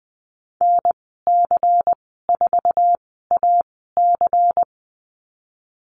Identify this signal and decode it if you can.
Morse code, heard as a keyed tone, about 20 wpm, spaced as for 15 wpm, 700 Hz. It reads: NC4AC
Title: 122 Lesson 20g - Number 3 - Call Signs 20wpm